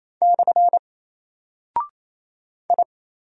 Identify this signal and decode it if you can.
Morse code, heard as a keyed tone, about 28 wpm, 700 Hz. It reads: / I
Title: Single Character (Letter:Number:Punctuation:Prosign) Alt 28wpm